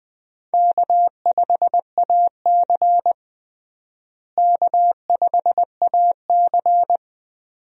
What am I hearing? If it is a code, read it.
Morse code, heard as a keyed tone, about 20 wpm, 700 Hz. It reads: K5AC K5AC
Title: USA Call Signs - 4 Characters - Repeated 2x - 20wpm